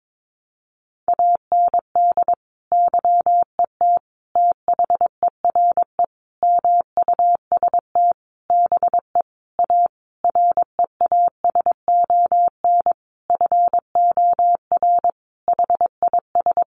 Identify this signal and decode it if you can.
Morse code, heard as a keyed tone, about 22 wpm, 700 Hz. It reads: AND YET THERE MUST BE A REASON FOR HIS
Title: Sentences from Top 1300 Words 22wpm